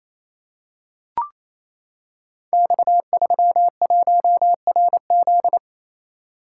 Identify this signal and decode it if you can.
Morse code, heard as a keyed tone, about 28 wpm, 700 Hz. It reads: X31RZ
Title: Code Groups - 5 Letter:Numbers - 28wpm